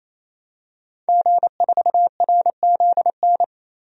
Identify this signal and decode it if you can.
Morse code, heard as a keyed tone, about 28 wpm, 700 Hz. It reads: G4RZN
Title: DXCC-100 Call Signs - 5 Characters - 28wpm